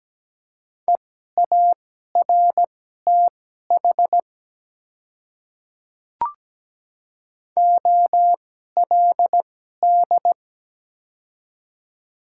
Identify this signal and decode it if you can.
Morse code, heard as a keyed tone, about 17 wpm, spaced as for 12 wpm, 700 Hz. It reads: EARTH OLD
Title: 2x Character Spacing - Top 200 Words - 17wpm